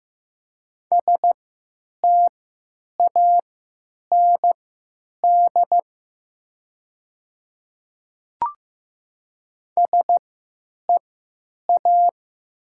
Morse code, heard as a keyed tone, about 15 wpm, spaced as for 9 wpm, 700 Hz. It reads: STAND SEA
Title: 3x Character Spacing - Top 300 Words - 15wpm